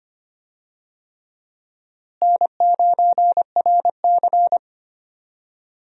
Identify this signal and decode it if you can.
Morse code, heard as a keyed tone, about 25 wpm, 700 Hz. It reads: N9RC
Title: USA Call Signs - 4 Characters - Repeated 3x - 25wpm